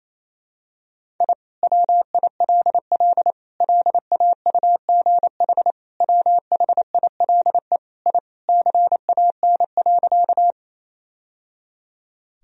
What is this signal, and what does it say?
Morse code, heard as a keyed tone, about 28 wpm, 700 Hz. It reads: I WILL LAUGH WHILE I CAN.
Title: Sentences from Top 1000 Words - Review Part 1 - 28wpm